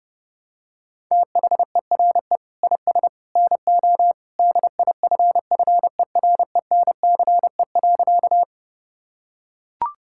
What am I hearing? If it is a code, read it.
Morse code, heard as a keyed tone, about 30 wpm, 700 Hz. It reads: THERE IS NO DIFFERENCE.
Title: Sentences from Top 1000 Words - Review Part 5 - 30wpm